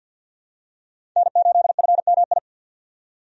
Morse code, heard as a keyed tone, about 48 wpm, 700 Hz. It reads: N8FMI
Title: USA Call Signs - 5 Characters - 50wpm